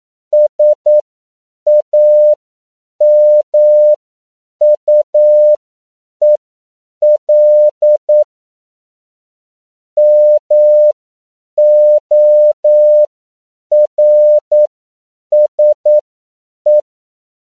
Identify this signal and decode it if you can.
Morse code, heard as a keyed tone, about 9 wpm, spaced as for 7 wpm, 600 Hz. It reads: SAMUEL MORSE
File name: CW / 10PPM